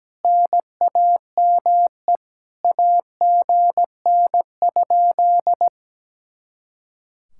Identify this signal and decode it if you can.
Morse code, heard as a keyed tone, about 17 wpm, 700 Hz. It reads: NAME AGN?
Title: Speed-Racing Common QSO Phrases Encore - 17wpm